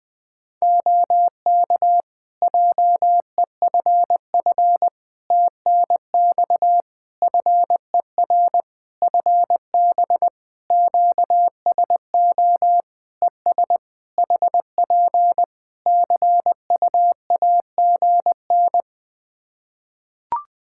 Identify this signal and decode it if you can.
Morse code, heard as a keyed tone, about 21 wpm, 700 Hz. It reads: OK JEFF TNX FER FB QSO ES HP CUAGN
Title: K1USN SST QSO Practice 20wpm